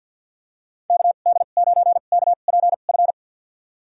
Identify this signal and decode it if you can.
Morse code, heard as a keyed tone, about 47 wpm, spaced as for 35 wpm, 700 Hz. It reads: KD9KPF
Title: USA Call Signs - 2x Character Spacing - 50wpm